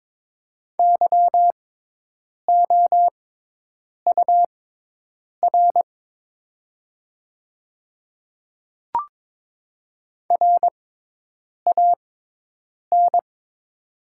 Morse code, heard as a keyed tone, about 22 wpm, spaced as for 8 wpm, 700 Hz. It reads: YOUR RAN
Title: 6x Character Spacing - Top 500 Words - 22wpm